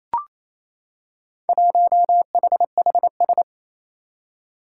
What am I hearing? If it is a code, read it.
Morse code, heard as a keyed tone, about 28 wpm, 700 Hz. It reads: 1HHS
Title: Four Challenging Letter-Number Combinations 28wpm